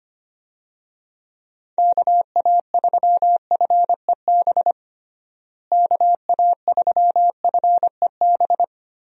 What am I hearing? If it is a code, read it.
Morse code, heard as a keyed tone, about 25 wpm, 700 Hz. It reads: KA3FEB KA3FEB
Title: USA Call Signs - 6 Characters - Repeated 3x - 25wpm